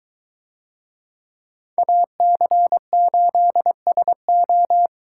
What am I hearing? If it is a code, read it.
Morse code, heard as a keyed tone, about 22 wpm, 700 Hz. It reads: AC8SO